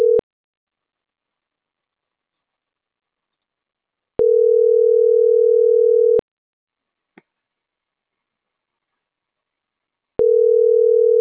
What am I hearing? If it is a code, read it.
Ringback tone (North America), 440 + 480 Hz.